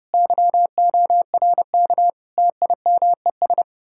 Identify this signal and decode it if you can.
Morse code, heard as a keyed tone, about 39 wpm, 700 Hz. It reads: YORK TIMES